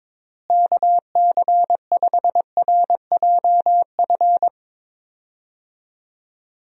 Morse code, heard as a keyed tone, about 22 wpm, 700 Hz. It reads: KC5RJF
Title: USA Call Signs - 6 Characters - Repeated 2x - 22wpm